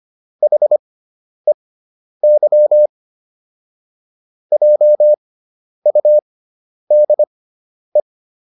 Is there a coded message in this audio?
Yes: Morse code from 0.4 to 8.0 s, heard as a keyed tone, about 25 wpm, spaced as for 10 wpm, 600 Hz. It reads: HEY JUDE